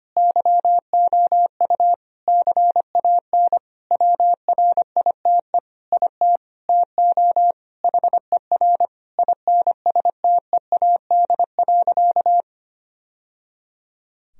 Morse code, heard as a keyed tone, about 25 wpm, 700 Hz. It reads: YOU CAN WRITE IT TO HER INSTEAD.
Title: Sentences from Top 1000 Words - Review Part 4 - 25wpm